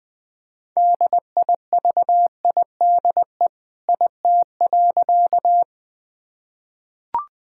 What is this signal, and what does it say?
Morse code, heard as a keyed tone, about 20 wpm, 700 Hz. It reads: DIVIDE IT.
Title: Sentences from Top 1600 Words 20wpm